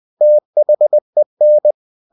Morse code, heard as a keyed tone, about 20 wpm, 600 Hz. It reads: THEN
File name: the old man and the sea part 2 20wpm